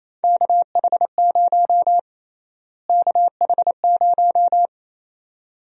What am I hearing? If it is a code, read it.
Morse code, heard as a keyed tone, about 28 wpm, 700 Hz. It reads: KH0 KH0
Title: USA Call Sign Prefixes - Repeated 2x - 28wpm